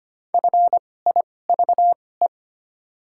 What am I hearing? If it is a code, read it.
Morse code, heard as a keyed tone, about 25 wpm, spaced as for 18 wpm, 700 Hz. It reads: FIVE
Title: 2x Character Spacing - Top 500 Words - 25wpm